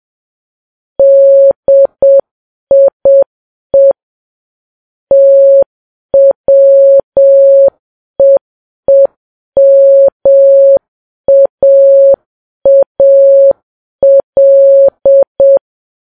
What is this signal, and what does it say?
Morse code, heard as a keyed tone, about 7 wpm, 550 Hz. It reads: DIE TWEEMAAL
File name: Files Web tls / Morse on9fv / morse7kps